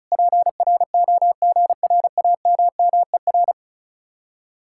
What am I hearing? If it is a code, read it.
Morse code, heard as a keyed tone, about 45 wpm, 700 Hz. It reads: PROGRAMMER